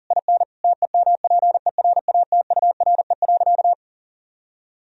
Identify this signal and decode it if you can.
Morse code, heard as a keyed tone, about 40 wpm, 700 Hz. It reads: IN TEMPERATURE.
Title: Sentences from Top 1000 Words - Review Part 4 - 40wpm